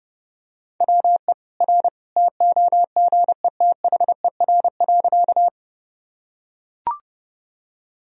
Morse code, heard as a keyed tone, about 30 wpm, 700 Hz. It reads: WE R TOGETHER.
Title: Sentences from Top 300 Words and Abbreviations 30wpm